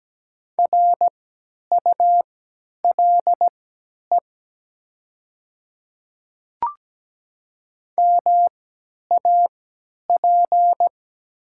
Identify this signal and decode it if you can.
Morse code, heard as a keyed tone, about 17 wpm, spaced as for 10 wpm, 700 Hz. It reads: RULE MAP
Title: 3x Character Spacing - Top 500 Words - 17wpm